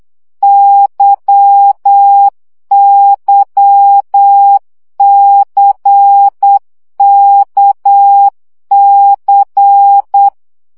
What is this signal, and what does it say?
Morse code, heard as a keyed tone, about 8 wpm, 800 Hz. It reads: YYCKC
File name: CW gyakorlas k c y 7 baud